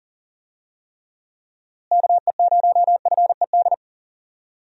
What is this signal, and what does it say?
Morse code, heard as a keyed tone, about 40 wpm, 700 Hz. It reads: KE0FED
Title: USA Call Signs - 6 Characters - Repeated 3x - 40wpm